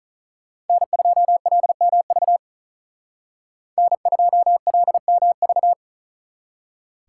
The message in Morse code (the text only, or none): N2LMV N2LMV